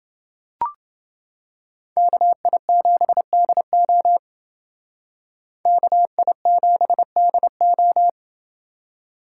Morse code, heard as a keyed tone, about 28 wpm, 700 Hz. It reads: KI7DO KI7DO